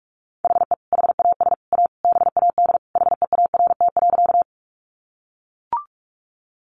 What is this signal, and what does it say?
Morse code, heard as a keyed tone, about 44 wpm, 700 Hz. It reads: HE HAS A BAD HEART.
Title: Sentences from Top 1200 Words 45wpm